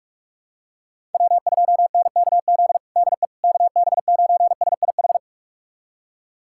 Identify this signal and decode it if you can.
Morse code, heard as a keyed tone, about 46 wpm, 700 Hz. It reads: W2NKZ DE KB9SIH
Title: USA Callsign DE Callsign 45wpm